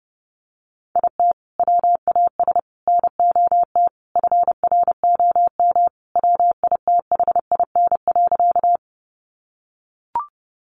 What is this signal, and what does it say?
Morse code, heard as a keyed tone, about 28 wpm, 700 Hz. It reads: IT WAS NOT FROM WITHIN.
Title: Sentences from Top 700 Words 30wpm